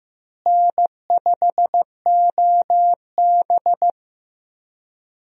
Morse code, heard as a keyed tone, about 15 wpm, 700 Hz. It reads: N5OB
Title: USA Call Signs - 4 Characters - Repeated 2x - 15wpm